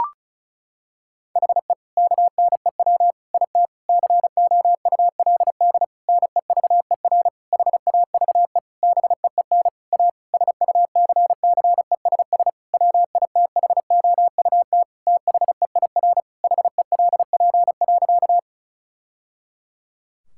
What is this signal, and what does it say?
Morse code, heard as a keyed tone, about 36 wpm, 700 Hz. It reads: HE KNEW IT COULD NEVER HAVE BEEN A SUCCESS WITHOUT THEIR HELP.
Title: Sentences from Top 1000 Words - Review Part 6 - 35wpm